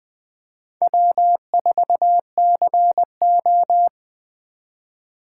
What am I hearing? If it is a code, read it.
Morse code, heard as a keyed tone, about 20 wpm, 700 Hz. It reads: W4CO